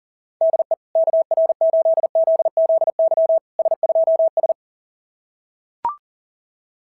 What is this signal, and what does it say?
Morse code, heard as a keyed tone, about 40 wpm, 690 Hz. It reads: DE KR8ZZY S2S
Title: SOTA QSO Practice 40wpm